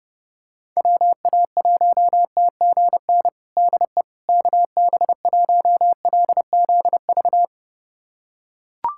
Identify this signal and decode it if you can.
Morse code, heard as a keyed tone, about 30 wpm, 700 Hz. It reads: WA1TGN DE KB1LZV
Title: USA Callsign DE Callsign - 6 Characters - 30wpm